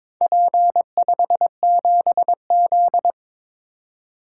Morse code, heard as a keyed tone, about 22 wpm, 700 Hz. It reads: P57Z
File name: practice / 22wpm / Four Letter-Number Combinations - 22wpm